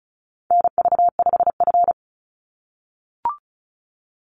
Morse code, heard as a keyed tone, about 34 wpm, 700 Hz. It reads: NV5F
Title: Call Signs - 35wpm